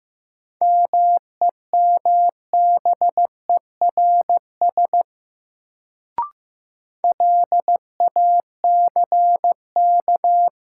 Morse code, heard as a keyed tone, about 15 wpm, 700 Hz. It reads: MEMBERS LACK
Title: Sets of 3 Words - New Word Drill - 15wpm